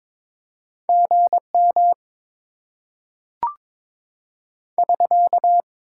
Morse code, heard as a keyed tone, about 22 wpm, 700 Hz. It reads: GM <SK>
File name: Top 90 QSO Elements - 6x Character Spacing - 22wpm